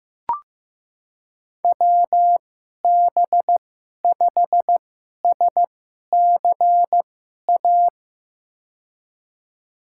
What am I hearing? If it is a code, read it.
Morse code, heard as a keyed tone, about 15 wpm, spaced as for 11 wpm, 700 Hz. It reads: WB5SCA
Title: USA Call Signs - 6 Characters - 2x Character Spacing - 15wpm